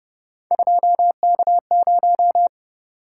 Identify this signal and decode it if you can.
Morse code, heard as a keyed tone, about 30 wpm, 700 Hz. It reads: 2K0